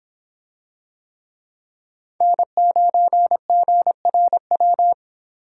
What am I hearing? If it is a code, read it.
Morse code, heard as a keyed tone, about 25 wpm, 700 Hz. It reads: N9GRW